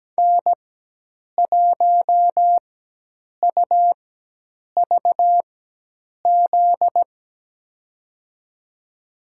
Morse code, heard as a keyed tone, about 17 wpm, spaced as for 8 wpm, 700 Hz. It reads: N1UVZ
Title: USA Call Signs - 5 Characters - 4x Character Spacing - 17wpm